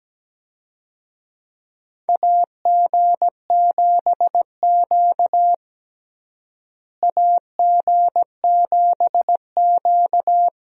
Morse code, heard as a keyed tone, about 17 wpm, 700 Hz. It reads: AG7Q AG7Q
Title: USA Call Signs - 4 Characters - Repeated 2x - 17wpm